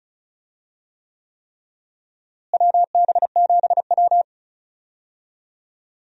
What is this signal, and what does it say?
Morse code, heard as a keyed tone, about 36 wpm, 700 Hz. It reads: WB7W
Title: USA Call Signs - 4 Characters - Repeated 3x - 35wpm